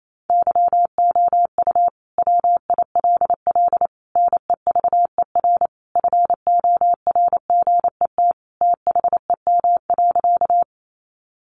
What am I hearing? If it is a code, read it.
Morse code, heard as a keyed tone, about 28 wpm, 700 Hz. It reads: YOU WILL NEVER FORGET THEM.
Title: Sentences from Top 1200 Words 28wpm